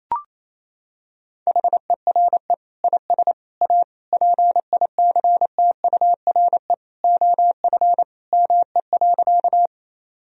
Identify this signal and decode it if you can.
Morse code, heard as a keyed tone, about 28 wpm, 700 Hz. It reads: HERE IS A PICTURE OF ME.
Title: Sentences from Top 1000 Words - Review Part 1 - 28wpm